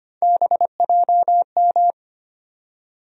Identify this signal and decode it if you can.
Morse code, heard as a keyed tone, about 25 wpm, 700 Hz. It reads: BJM
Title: USA Call Sign Postfixes - 3 Letters 25wpm